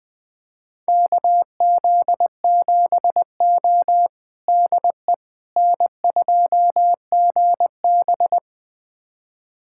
Morse code, heard as a keyed tone, about 20 wpm, 700 Hz. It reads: KZ7O DE N2GB
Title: USA Callsign DE Callsign - 4 Characters - 20wpm